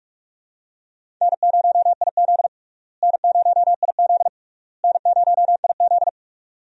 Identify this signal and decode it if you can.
Morse code, heard as a keyed tone, about 46 wpm, 700 Hz. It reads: N0IZ N0IZ N0IZ
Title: USA Call Signs - 4 Characters - Repeated 3x - 45wpm